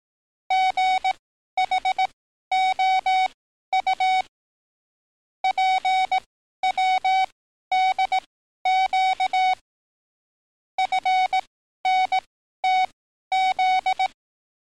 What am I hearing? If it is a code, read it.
Morse code, heard as a keyed tone, about 18 wpm, spaced as for 12 wpm, 750 Hz. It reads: GHOU PWDQ FNTZ
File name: training / NAVPERS 11049to11060 / NAVPERS 11053 RA RB